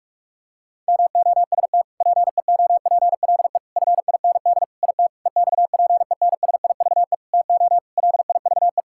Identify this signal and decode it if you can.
Morse code, heard as a keyed tone, about 46 wpm, 700 Hz. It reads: MOST PEOPLE FIND IT EXPENSIVE TO LIVE
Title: Sentences from Top 1000 Words - Review Part 6 - 45wpm